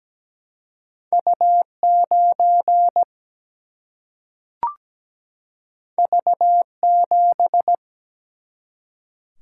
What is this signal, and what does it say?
Morse code, heard as a keyed tone, about 17 wpm, 700 Hz. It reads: U9 V7
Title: Single-Letter Single-Number 17wpm